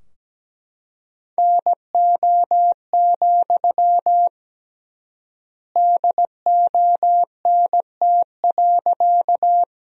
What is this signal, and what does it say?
Morse code, heard as a keyed tone, about 17 wpm, 700 Hz. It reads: NO, DONT.
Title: 3x Word-Spacing - Sentences from Top 300 Words - 17wpm